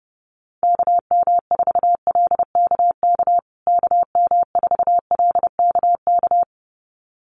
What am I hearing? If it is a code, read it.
Morse code, heard as a keyed tone, about 30 wpm, 700 Hz. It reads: KM4LKK KM4LKK
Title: USA Call Signs Repeated 2x 30wpm